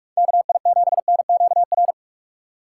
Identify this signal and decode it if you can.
Morse code, heard as a keyed tone, about 46 wpm, 700 Hz. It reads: KI7NQR
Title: USA Call Signs - 6 Characters - 6x Character Spacing - 45wpm